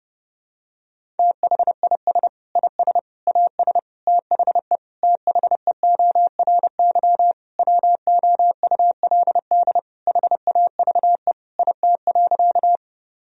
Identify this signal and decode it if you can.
Morse code, heard as a keyed tone, about 32 wpm, 700 Hz. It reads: THIS IS AS THE THEORY WOULD HAVE IT.
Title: Sentences from Top 1000 Words - Review Part 3 - 30wpm